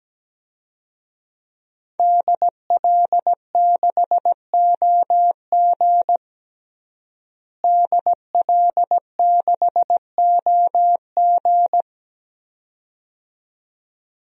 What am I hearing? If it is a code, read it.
Morse code, heard as a keyed tone, about 17 wpm, 700 Hz. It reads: DL6OG DL6OG
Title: DXCC-100 Call Signs - 5 Characters - Repeated 2x - 17wpm